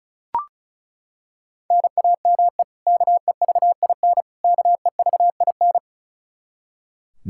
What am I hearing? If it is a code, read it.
Morse code, heard as a keyed tone, about 34 wpm, 700 Hz. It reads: NAME KEVIN KEVIN
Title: Practice QSO - Exchange 1 - Beginning - 35wpm